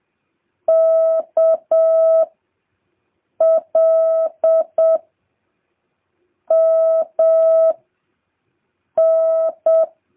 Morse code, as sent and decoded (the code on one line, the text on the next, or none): -.- / .-.. / -- / -.
K L M N